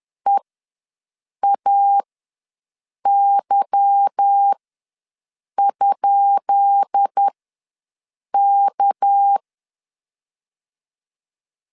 Morse code, heard as a keyed tone, about 10 wpm, spaced as for 6 wpm, 800 Hz. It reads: EAY?K
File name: LEZIONI / 10 SIGNS / 10B INT ws 10wpm 4134